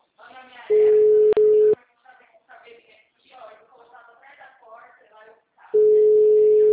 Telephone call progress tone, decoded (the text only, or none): ringback tone